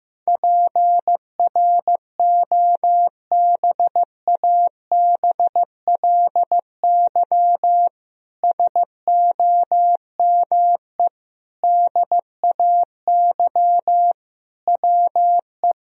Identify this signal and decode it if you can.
Morse code, heard as a keyed tone, about 15 wpm, 700 Hz. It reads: PROBABLY SOME DAY WE
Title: Sentences from Top 1000 Words - Review Part 4 - 15wpm